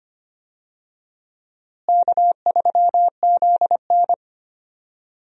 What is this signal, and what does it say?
Morse code, heard as a keyed tone, about 25 wpm, 700 Hz. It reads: K3ZN
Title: USA Call Signs - 4 Characters - Repeated 2x - 25wpm